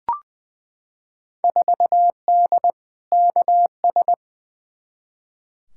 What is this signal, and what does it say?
Morse code, heard as a keyed tone, about 21 wpm, 700 Hz. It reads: 4D KS
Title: Field Day Exchanges Encore - 20wpm